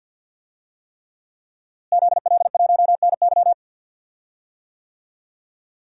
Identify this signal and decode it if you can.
Morse code, heard as a keyed tone, about 50 wpm, 700 Hz. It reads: ZL1NY
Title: DXCC-100 Call Signs - 5 Characters - Repeated 3x - 50wpm